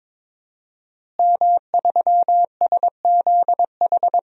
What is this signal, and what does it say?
Morse code, heard as a keyed tone, about 22 wpm, 700 Hz. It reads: M3SZH
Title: DXCC-100 Call Signs - 5 Characters - Repeated 3x - 22wpm